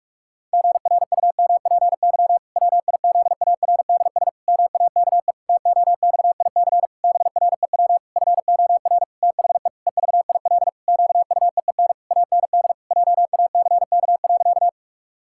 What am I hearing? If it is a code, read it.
Morse code, heard as a keyed tone, about 46 wpm, 700 Hz. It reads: GRUMPY WIZARDS MAKE TOXIC BREW FOR THE EVIL QUEEN AND JACK.
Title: Pangram Challenge v2 - 45wpm